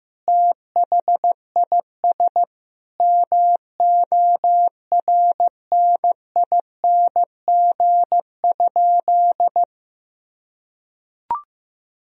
Morse code, heard as a keyed tone, about 15 wpm, 700 Hz. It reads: THIS MORNING?
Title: Sentences from Top 1000 Words - Review Part 2 - 15wpm